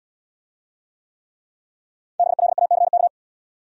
Morse code, heard as a keyed tone, about 73 wpm, 700 Hz. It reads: BLACK